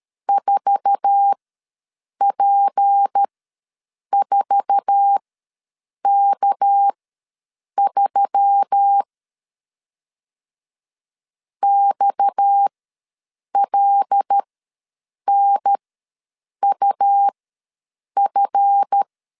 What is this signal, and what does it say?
Morse code, heard as a keyed tone, about 13 wpm, spaced as for 7 wpm, 800 Hz. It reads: 4P4K3 XLNUF
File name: LEZIONI / 8 YJ45 / 8 YJ4 ws 12wpm 3249